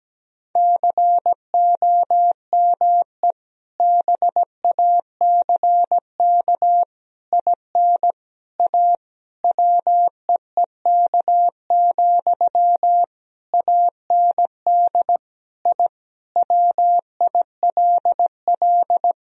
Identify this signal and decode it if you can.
Morse code, heard as a keyed tone, about 17 wpm, 700 Hz. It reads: COME BACK IN A WEEK, AND I WILL
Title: Sentences from Top 1000 Words - Review Part 3 - 17wpm